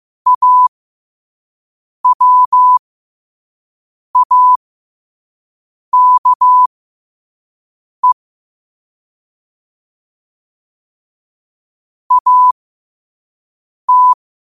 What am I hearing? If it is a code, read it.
Morse code, heard as a keyed tone, about 15 wpm, spaced as for 5 wpm, 1000 Hz. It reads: AWAKE AT
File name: qotdfiles / 5wpm